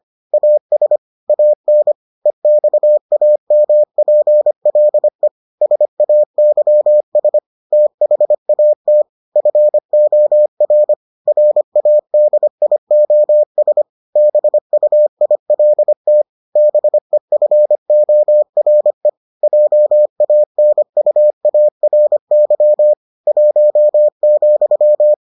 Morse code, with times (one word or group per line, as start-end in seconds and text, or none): 0.3-1.0 AS
1.3-1.9 AN
2.3-5.3 EXAMPLE
5.6-7.4 SAYS
7.7-9.0 THAT
9.4-10.9 FOR
11.3-13.8 RADIOS
14.2-16.2 BUILT
16.6-19.1 BEFORE
19.4-22.9 JANUARY
23.3-25.1 1,